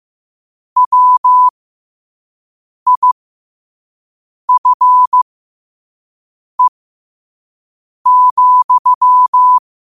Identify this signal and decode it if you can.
Morse code, heard as a keyed tone, about 15 wpm, spaced as for 5 wpm, 1000 Hz. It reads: WIFE,
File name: qotdfiles / 5wpm